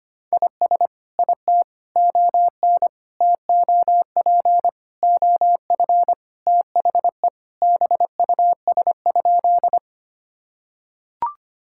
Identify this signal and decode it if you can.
Morse code, heard as a keyed tone, about 25 wpm, 700 Hz. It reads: IS IT ON TOP OF THE BUS?
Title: Sentences from Top 1000 Words - Review Part 6 - 25wpm